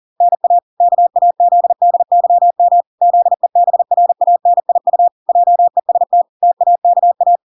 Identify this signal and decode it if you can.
Morse code, heard as a keyed tone, about 40 wpm, 700 Hz. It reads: NA KAZDYM ZEBRANIU JEST TAKA
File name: 40wpm